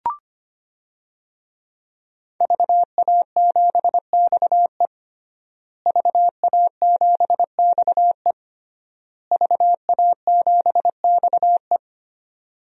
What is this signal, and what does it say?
Morse code, heard as a keyed tone, about 25 wpm, 700 Hz. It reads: VA7XE VA7XE VA7XE
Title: DXCC-100 Call Signs - 5 Characters - Repeated 3x - 25wpm